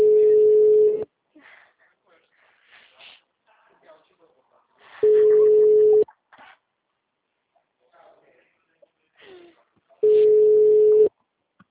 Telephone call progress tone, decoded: ringback tone

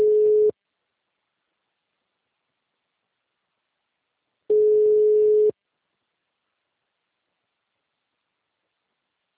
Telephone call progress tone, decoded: ringback tone

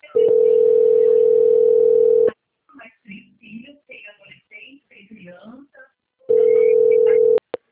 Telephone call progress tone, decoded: ringback tone

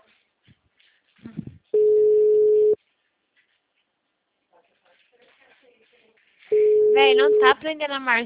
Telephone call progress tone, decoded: ringback tone

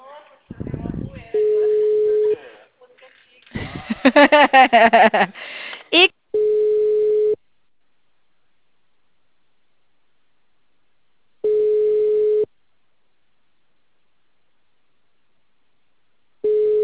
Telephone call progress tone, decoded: ringback tone

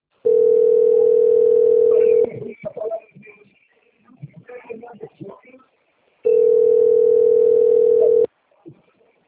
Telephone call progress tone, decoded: ringback tone

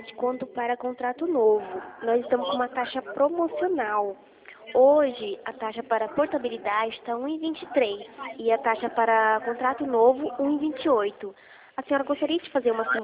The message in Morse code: none

